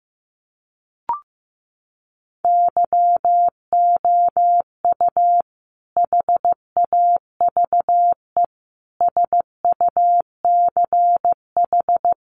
Morse code, heard as a keyed tone, about 15 wpm, 700 Hz. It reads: YOU HAVE SUCH
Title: Sentences from Top 1200 Words 15wpm